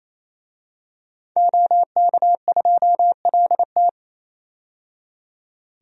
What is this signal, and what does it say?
Morse code, heard as a keyed tone, about 28 wpm, 700 Hz. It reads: OK2LT
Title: DXCC-100 Call Signs - 5 Characters - 28wpm